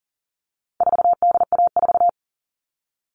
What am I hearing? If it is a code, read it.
Morse code, heard as a keyed tone, about 40 wpm, 700 Hz. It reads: 4DA4
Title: All Cut Numbers - 4-Digit 40wpm